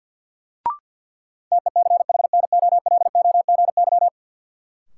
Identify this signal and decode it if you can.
Morse code, heard as a keyed tone, about 51 wpm, 700 Hz. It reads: TECHNOLOGY
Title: Sentences from Top 1200 Words - New Word Drill Alt - 50wpm